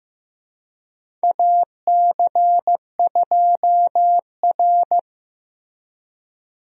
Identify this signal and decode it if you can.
Morse code, heard as a keyed tone, about 15 wpm, 700 Hz. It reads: AC2R
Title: USA Call Signs - 4 Characters - Repeated 2x - 15wpm